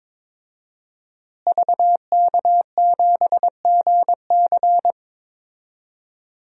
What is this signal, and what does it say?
Morse code, heard as a keyed tone, about 22 wpm, 700 Hz. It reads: VK7GC
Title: DXCC-100 Call Signs - 5 Characters - Repeated 3x - 22wpm